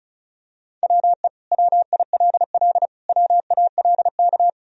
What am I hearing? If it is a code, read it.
Morse code, heard as a keyed tone, about 34 wpm, 700 Hz. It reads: WE WILL WALK
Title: Sentences from Top 1000 Words - Review Part 1 - 35wpm